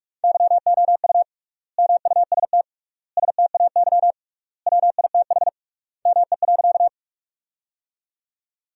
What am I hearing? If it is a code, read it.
Morse code, heard as a keyed tone, about 46 wpm, 700 Hz. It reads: YOU MUST STAY WITH ME.
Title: 3x Word-Spacing - Sentences from Top 500 Words - 45wpm